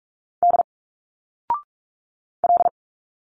Morse code, heard as a keyed tone, about 46 wpm, 700 Hz. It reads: D L
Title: Single Letters 45wpm